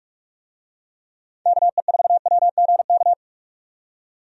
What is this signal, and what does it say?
Morse code, heard as a keyed tone, about 42 wpm, 700 Hz. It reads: KE4WGK
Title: USA Call Signs - 6 Characters - 45wpm